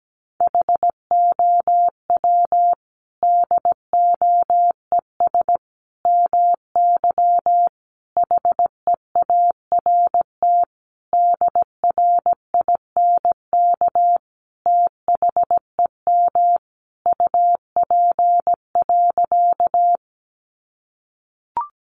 Morse code, heard as a keyed tone, about 17 wpm, 700 Hz. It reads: HOW DOES MY HEART DRINK THEM UP.
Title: Sentences from Top 1500 Words 17wpm